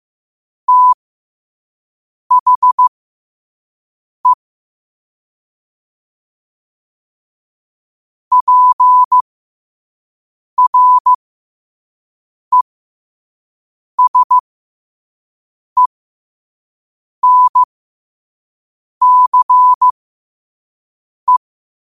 Morse code, heard as a keyed tone, about 15 wpm, spaced as for 5 wpm, 1000 Hz. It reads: THE PRESENCE
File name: qotdfiles / 5wpm